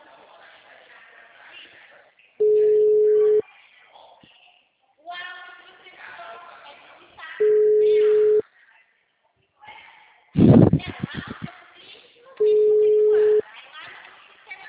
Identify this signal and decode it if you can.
Ringback tone (Europe, 425 Hz), with speech.